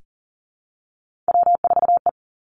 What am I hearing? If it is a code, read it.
Morse code, heard as a keyed tone, about 40 wpm, 700 Hz. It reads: W4E
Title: Three Challenging Letter-Number Combinations 40wpm